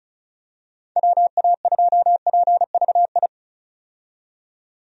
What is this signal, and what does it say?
Morse code, heard as a keyed tone, about 36 wpm, 700 Hz. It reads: WA2PVI
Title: USA Call Signs - 6 Characters - 35wpm